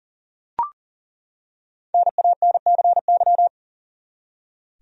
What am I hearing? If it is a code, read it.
Morse code, heard as a keyed tone, about 40 wpm, 700 Hz. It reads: NANCY
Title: USA Common Names - Max 5 Letters - 40wpm